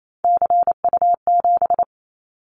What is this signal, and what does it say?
Morse code, heard as a keyed tone, about 28 wpm, 700 Hz. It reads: CU7
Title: Callsign Prefixes for Top 50 DXCC - 28wpm